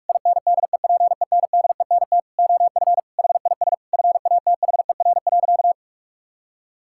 Morse code, heard as a keyed tone, about 41 wpm, 700 Hz. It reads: INDEPENDENT OF HIS FATHER.
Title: Sentences from Top 1000 Words - Review Part 6 - 45wpm